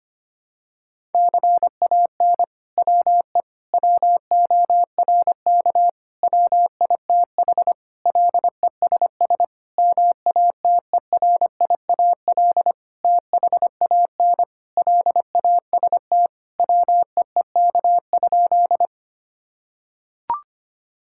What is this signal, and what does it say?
Morse code, heard as a keyed tone, about 25 wpm, 700 Hz. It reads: CAN WE WORK WITH LESS MATERIAL THAN LAST WEEK?
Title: Sentences from Top 1000 Words - Review Part 5 - 25wpm